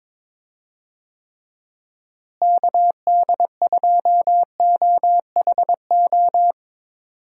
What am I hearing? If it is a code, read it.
Morse code, heard as a keyed tone, about 21 wpm, 700 Hz. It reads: KD2OHO
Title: USA Call Signs - 6 Characters - Repeated 2x - 22wpm